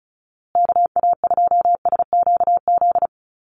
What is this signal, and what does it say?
Morse code, heard as a keyed tone, about 34 wpm, 700 Hz. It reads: KA2SQZ
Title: USA Call Signs Repeated 2x 35wpm